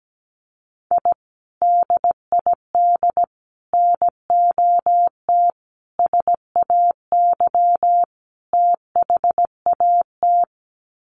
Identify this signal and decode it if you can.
Morse code, heard as a keyed tone, about 17 wpm, 700 Hz. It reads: I DID NOT SAY THAT
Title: Sets of 5 Words Encore2 17wpm